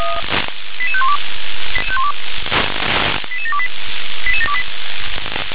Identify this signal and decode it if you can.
Five-tone selective call (ZVEI), heard as quick runs of five tones, tones of about 70 ms. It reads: 88512 88728 88728